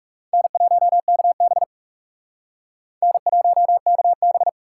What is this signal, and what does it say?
Morse code, heard as a keyed tone, about 36 wpm, 700 Hz. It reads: N1KB N1KB
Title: Speed Racing - USA 1x2 Call Signs 30wpm